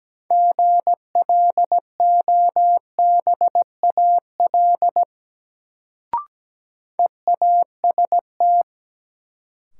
Morse code, heard as a keyed tone, about 17 wpm, 700 Hz. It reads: GLOBAL EAST
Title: Sentences from Top 1200 Words - New Word Drill Alt - 17wpm